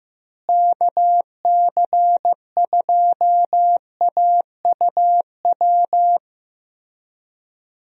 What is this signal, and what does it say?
Morse code, heard as a keyed tone, about 15 wpm, 700 Hz. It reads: KC2AUW
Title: USA Call Signs - 6 Characters - Repeated 2x - 15wpm